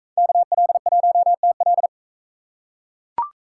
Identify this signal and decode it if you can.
Morse code, heard as a keyed tone, about 43 wpm, 700 Hz. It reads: KL1TL